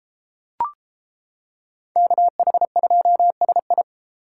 Morse code, heard as a keyed tone, about 31 wpm, 700 Hz. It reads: KH2SI